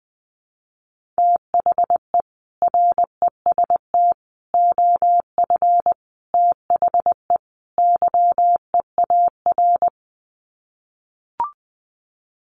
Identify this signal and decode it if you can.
Morse code, heard as a keyed tone, about 20 wpm, 700 Hz. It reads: THE REST OF THE YEAR
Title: Sets of 5 Words Encore 20wpm